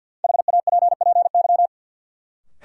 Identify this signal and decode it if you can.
Morse code, heard as a keyed tone, about 53 wpm, 700 Hz. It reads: HAPPY
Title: Sentences from Top 1000 Words - New Word Drill - 50wpm